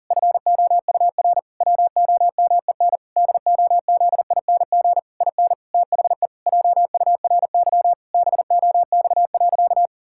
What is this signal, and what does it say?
Morse code, heard as a keyed tone, about 37 wpm, 700 Hz. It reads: FOUR WOMEN DOZING IN THE JURY BOX.
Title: Pangram Challenge v2 - 40wpm